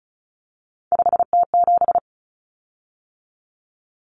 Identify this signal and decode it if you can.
Morse code, heard as a keyed tone, about 34 wpm, 700 Hz. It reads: 5T7
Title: Common Cut Numbers - 3-Digit 35wpm